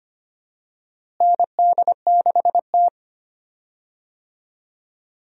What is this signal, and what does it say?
Morse code, heard as a keyed tone, about 25 wpm, 700 Hz. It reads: ND6T
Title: USA Call Signs - 4 Characters - Repeated 2x - 25wpm